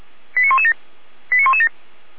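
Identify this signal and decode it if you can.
Five-tone selective call (ZVEI), heard as quick runs of five tones, tones of about 80 ms. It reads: 89117 89117
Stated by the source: ZVEI five-tone selective call received on SDR.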